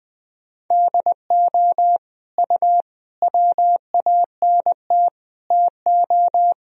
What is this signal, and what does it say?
Morse code, heard as a keyed tone, about 20 wpm, 700 Hz. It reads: DO U WANT TO
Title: Sentences from Top 300 Words and Abbreviations 20wpm